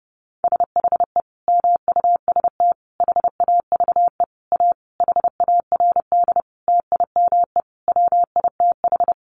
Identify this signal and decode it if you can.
Morse code, heard as a keyed tone, about 30 wpm, 700 Hz. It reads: SHE MUST HAVE A HARD TIME WITH
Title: Sentences from Top 300 Words 30wpm